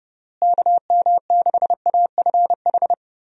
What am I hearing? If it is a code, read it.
Morse code, heard as a keyed tone, about 30 wpm, 700 Hz. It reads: KM6AFH
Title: USA Call Signs - 6 Characters - Repeated 2x - 30wpm